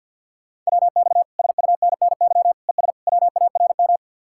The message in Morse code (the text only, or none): WX SUNNY ES WARM